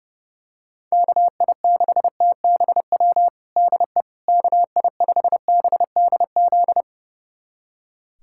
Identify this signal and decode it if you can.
Morse code, heard as a keyed tone, about 32 wpm, 700 Hz. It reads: KI6TBW DE KI5BDZ
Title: USA Callsign DE Callsign - 6 Characters - 30wpm